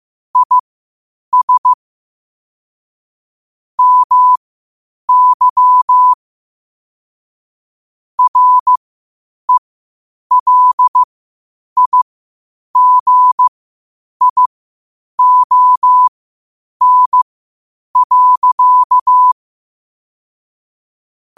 Morse code, heard as a keyed tone, about 15 wpm, spaced as for 8 wpm, 1000 Hz. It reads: IS MY RELIGION.
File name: qotdfiles / 8wpm